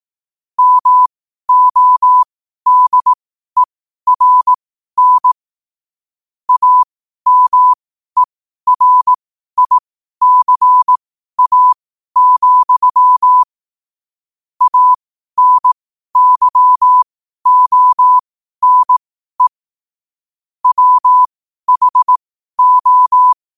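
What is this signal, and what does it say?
Morse code, heard as a keyed tone, about 18 wpm, spaced as for 12 wpm, 1000 Hz. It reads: MODERN AMERICA, ANYONE WHO
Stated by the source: Morse code quotes at 12 words per minute.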